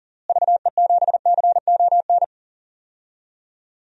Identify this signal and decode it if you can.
Morse code, heard as a keyed tone, about 40 wpm, 700 Hz. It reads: VE7CON
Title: DXCC-100 Call Signs - 6 Characters - Repeated 2x - 40wpm